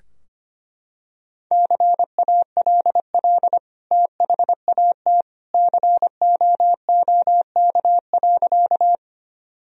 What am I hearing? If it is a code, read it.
Morse code, heard as a keyed tone, about 25 wpm, 700 Hz. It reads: CALL THAT COOK.
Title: Sentences from Top 1000 Words - Review Part 4 - 25wpm